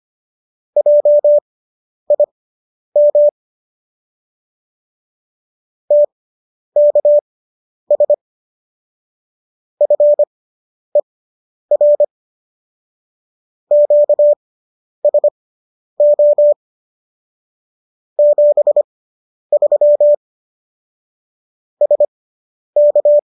Morse code, heard as a keyed tone, about 25 wpm, spaced as for 10 wpm, 590 Hz. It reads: JIM TKS FER QSO 73 SK